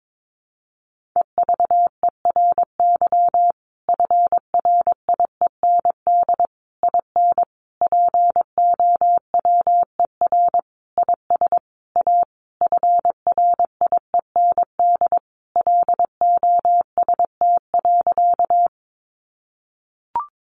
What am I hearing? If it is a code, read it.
Morse code, heard as a keyed tone, about 22 wpm, 700 Hz. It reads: EVERY FRIEND IN POWER IS A FRIEND LOST.
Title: Sentences from Top 1400 Words 22wpm